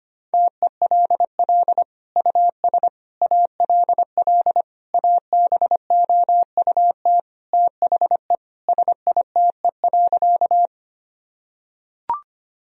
Morse code, heard as a keyed tone, about 25 wpm, 700 Hz. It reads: TELL US ALL ABOUT THE SITE.
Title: Sentences from Top 1000 Words - Review Part 4 - 25wpm